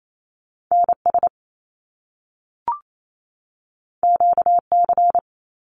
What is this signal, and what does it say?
Morse code, heard as a keyed tone, about 28 wpm, 700 Hz. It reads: NS QC
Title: Canadian Provinces and Territories 28wpm